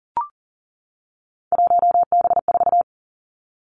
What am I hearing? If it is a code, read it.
Morse code, heard as a keyed tone, about 40 wpm, 700 Hz. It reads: 1B4
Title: Three Challenging Letter-Number Combinations 40wpm